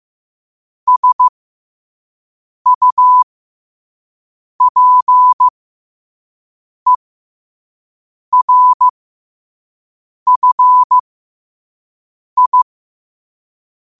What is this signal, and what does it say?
Morse code, heard as a keyed tone, about 15 wpm, spaced as for 5 wpm, 1000 Hz. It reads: SUPERFI
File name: qotdfiles / 5wpm